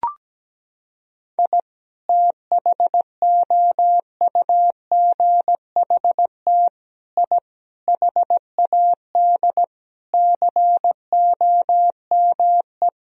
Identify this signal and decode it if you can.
Morse code, heard as a keyed tone, about 17 wpm, 700 Hz. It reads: I THOUGHT I HAD COME
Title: Sentences from Top 1000 Words - Review Part 4 - 17wpm